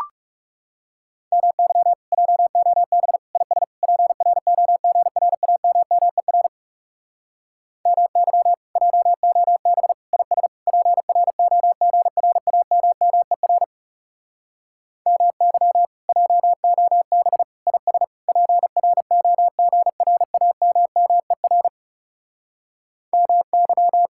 Morse code, heard as a keyed tone, about 37 wpm, 700 Hz. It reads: MY JOB IS PROGRAMMER MY JOB IS PROGRAMMER MY JOB IS PROGRAMMER MY
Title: Speed-Racing Common QSO Phrases - 30wpm